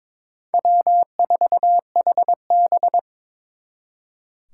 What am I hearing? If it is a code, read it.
Morse code, heard as a keyed tone, about 26 wpm, 700 Hz. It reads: W4HB
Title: Speed Racing - USA 1x2 Call Signs 22wpm